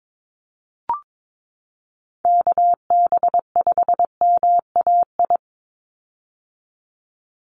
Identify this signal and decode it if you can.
Morse code, heard as a keyed tone, about 22 wpm, 700 Hz. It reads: KB5MAI
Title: USA Call Signs 22wpm